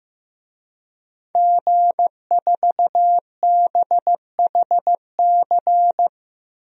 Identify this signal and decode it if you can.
Morse code, heard as a keyed tone, about 15 wpm, 700 Hz. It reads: G4BHC
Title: DXCC-100 Call Signs - 5 Characters - 15wpm